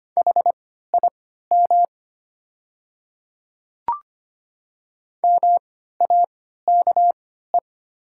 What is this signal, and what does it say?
Morse code, heard as a keyed tone, about 25 wpm, spaced as for 14 wpm, 700 Hz. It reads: HIM MAKE
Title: 3x Character Spacing - Top 100 Words - 25wpm